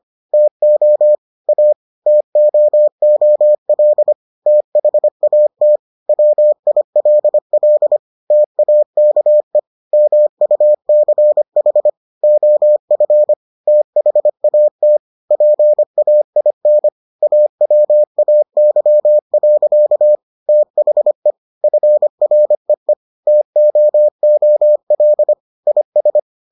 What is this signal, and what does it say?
Morse code, heard as a keyed tone, about 25 wpm, 600 Hz. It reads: TO A TOOL THAT WILL TAKE MUCH OF THAT PAIN AWAY. THE FREE TOOL IS